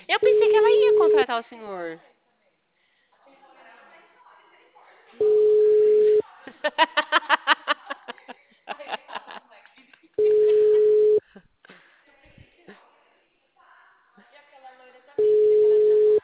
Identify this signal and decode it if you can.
Ringback tone (Europe, 425 Hz), with speech.